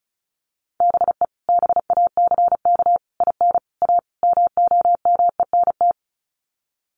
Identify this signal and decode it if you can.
Morse code, heard as a keyed tone, about 34 wpm, 700 Hz. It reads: BE BACK IN A MOMENT
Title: Sets of 5 Words 35wpm